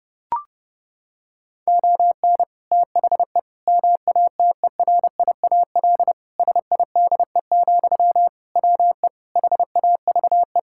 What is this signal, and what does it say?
Morse code, heard as a keyed tone, about 32 wpm, 700 Hz. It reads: ON THE MATERIAL SIDE, WE HAVE
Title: Sentences from Top 1000 Words - Review Part 5 - 30wpm